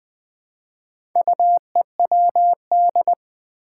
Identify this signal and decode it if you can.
Morse code, heard as a keyed tone, about 20 wpm, 700 Hz. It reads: UEWD
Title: Four Challenging Letter-Number Combinations 20wpm